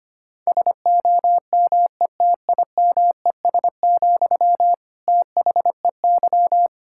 Morse code, heard as a keyed tone, about 25 wpm, 700 Hz. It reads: SOMETIMES, THEY
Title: Sentences from Top 1000 Words - Review Part 6 - 25wpm